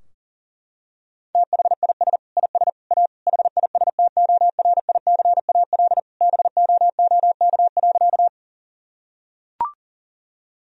Morse code, heard as a keyed tone, about 40 wpm, 700 Hz. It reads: THIS IS A HISTORICAL BOOK.
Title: Sentences from Top 1000 Words - Review Part 4 - 40wpm